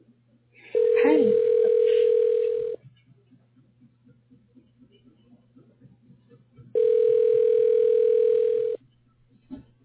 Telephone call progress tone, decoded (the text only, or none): ringback tone